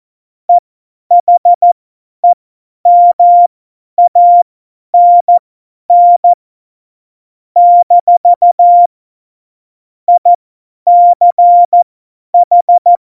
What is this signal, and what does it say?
Morse code, heard as a keyed tone, about 14 wpm, spaced as for 10 wpm, 700 Hz. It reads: EHEMANN - ICH
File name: Morsekurs / Witze / Witz 19 Mai 2020, 50 F70